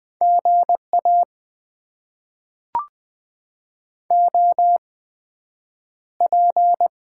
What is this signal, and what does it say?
Morse code, heard as a keyed tone, about 20 wpm, spaced as for 9 wpm, 700 Hz. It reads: GA OP
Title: Top 90 QSO Elements - 8x Character Spacing - 20wpm